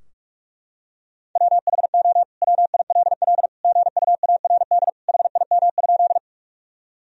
Morse code, heard as a keyed tone, about 41 wpm, 700 Hz. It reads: WHO WILL GUARD HIM?
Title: Sentences from Top 1000 Words - Review Part 4 - 45wpm